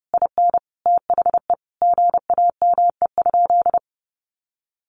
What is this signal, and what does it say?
Morse code, heard as a keyed tone, about 28 wpm, 700 Hz. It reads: IN THE GAME?
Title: Sentences from Top 1300 Words 30wpm